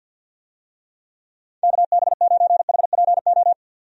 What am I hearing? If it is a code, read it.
Morse code, heard as a keyed tone, about 52 wpm, 700 Hz. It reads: KB9HPO
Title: USA Call Signs - 6 Characters - 50wpm